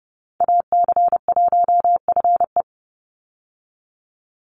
Morse code, heard as a keyed tone, about 30 wpm, 700 Hz. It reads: AC1FE
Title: Call Signs Repeated 3x - 30wpm